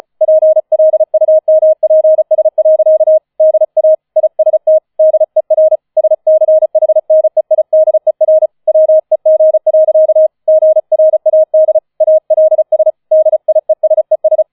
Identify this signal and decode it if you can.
Morse code, heard as a keyed tone, about 35 wpm, 630 Hz. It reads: PLUMPS. DA IST DER SCHNEIDER WEG. GRAD ALS DIESES